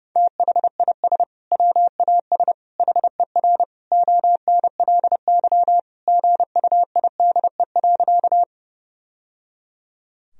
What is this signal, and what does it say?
Morse code, heard as a keyed tone, about 30 wpm, 700 Hz. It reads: THIS WAS HER ONLY GUIDE.
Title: Sentences from Top 1000 Words - Review Part 3 - 30wpm